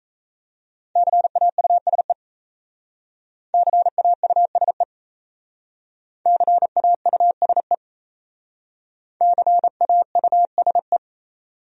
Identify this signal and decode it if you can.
Morse code, heard as a keyed tone, about 34 wpm, 700 Hz. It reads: CAUSE CAUSE CAUSE CAUSE
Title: Speed-Racing Top 300 Words 28wpm